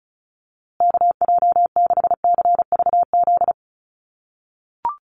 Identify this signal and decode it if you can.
Morse code, heard as a keyed tone, about 34 wpm, 700 Hz. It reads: KJ6CVZ
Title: USA Call Signs Repeated 2x 35wpm